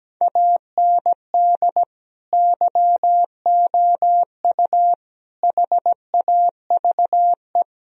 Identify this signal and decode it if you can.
Morse code, heard as a keyed tone, about 17 wpm, 700 Hz. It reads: AND YOU HAVE
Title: Sentences from Top 1000 Words - Review Part 1 - 17wpm